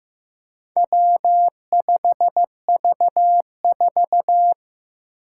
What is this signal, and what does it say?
Morse code, heard as a keyed tone, about 15 wpm, 700 Hz. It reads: W5V4
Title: Four Challenging Letter-Number Combinations 15wpm